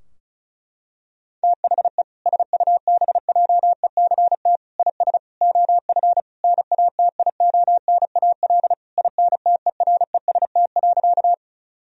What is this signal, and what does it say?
Morse code, heard as a keyed tone, about 34 wpm, 700 Hz. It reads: THE SUBJECT IS OF NATIONAL INTEREST.
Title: Sentences from Top 1000 Words - Review Part 4 - 35wpm